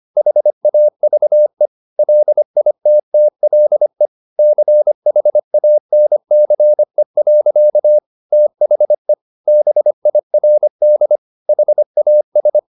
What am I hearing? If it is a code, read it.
Morse code, heard as a keyed tone, about 25 wpm, 600 Hz. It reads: HAVE LITTLE CHANCE. THE BIRD HAS
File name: the old man and the sea part 1 25wpm